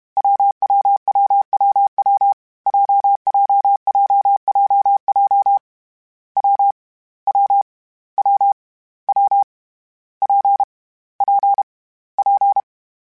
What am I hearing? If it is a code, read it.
Morse code, heard as a keyed tone, about 32 wpm, 800 Hz. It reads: WWWWW JJJJJ W W W W P P P
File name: Difficult Combos / Difficult Combos-mp3s / WPJ WPJ1 @30